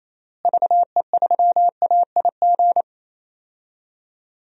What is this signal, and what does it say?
Morse code, heard as a keyed tone, about 28 wpm, 700 Hz. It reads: VE3AIG
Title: DXCC-100 Call Signs - 6 Characters - Repeated 3x - 28wpm